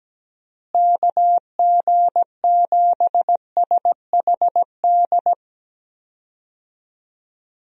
Morse code, heard as a keyed tone, about 17 wpm, 700 Hz. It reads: KG7SHD
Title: USA Call Signs - 6 Characters - Repeated 2x - 17wpm